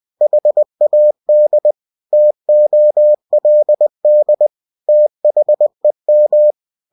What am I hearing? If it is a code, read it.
Morse code, heard as a keyed tone, about 20 wpm, 600 Hz. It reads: HAD TOLD THEM
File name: cw-books / winnie-the-pooh / winnie-20wpm-page-0103